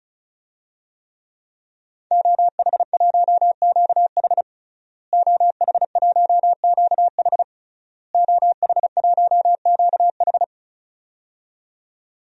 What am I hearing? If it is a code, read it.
Morse code, heard as a keyed tone, about 36 wpm, 700 Hz. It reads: OH1QH OH1QH OH1QH
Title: DXCC-100 Call Signs - 5 Characters - Repeated 3x - 35wpm